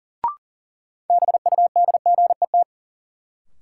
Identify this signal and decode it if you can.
Morse code, heard as a keyed tone, about 39 wpm, 700 Hz. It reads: BUDGET
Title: Sentences from Top 1200 Words - New Word Drill - 40wpm